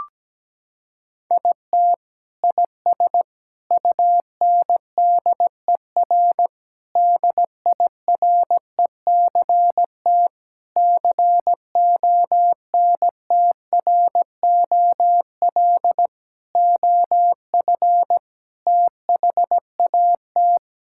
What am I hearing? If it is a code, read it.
Morse code, heard as a keyed tone, about 17 wpm, 700 Hz. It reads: IT IS UNDER DIRECT CONTROL OF THAT
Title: Sentences from Top 1000 Words - Review Part 5 - 17wpm